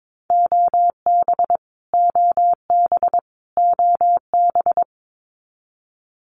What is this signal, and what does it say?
Morse code, heard as a keyed tone, about 22 wpm, 700 Hz. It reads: OB OB OB
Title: Abbreviations Repeated 3x - 22wpm